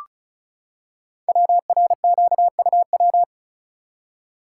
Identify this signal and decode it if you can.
Morse code, heard as a keyed tone, about 36 wpm, 690 Hz. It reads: WRQUW
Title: Code Groups - 5 Letters - 35wpm